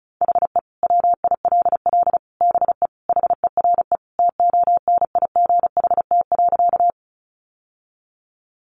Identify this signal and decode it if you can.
Morse code, heard as a keyed tone, about 34 wpm, 700 Hz. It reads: HE WILL BE HERE TONIGHT.
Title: Sentences from Top 600 Words 35wpm